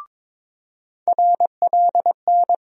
Morse code, heard as a keyed tone, about 22 wpm, 700 Hz. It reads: RLN
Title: USA Call Sign Postfixes - 3 Letters 22wpm